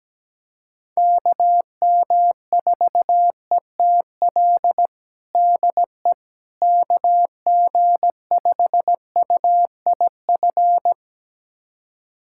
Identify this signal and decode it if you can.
Morse code, heard as a keyed tone, about 17 wpm, 700 Hz. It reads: KM4ETL DE KG5UIF
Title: USA Callsign DE Callsign - 6 Characters - 17wpm